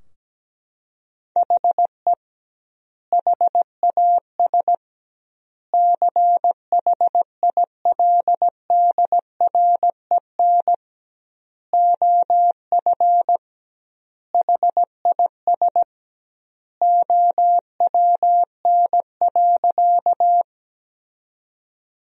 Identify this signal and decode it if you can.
Morse code, heard as a keyed tone, about 17 wpm, 700 Hz. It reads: HE HAS CHILDREN OF HIS OWN.
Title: 2x Word-Spacing - Sentences from Top 300 Words - 17wpm